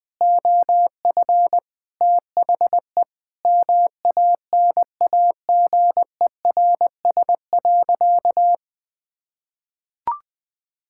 Morse code, heard as a keyed tone, about 20 wpm, 700 Hz. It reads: OF THE MANAGERS.
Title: Sentences with 6-Word Phrases - 20wpm